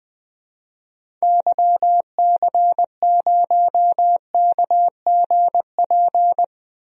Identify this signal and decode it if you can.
Morse code, heard as a keyed tone, about 20 wpm, 700 Hz. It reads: YC0KGP
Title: DXCC-100 Call Signs - 6 Characters - 20wpm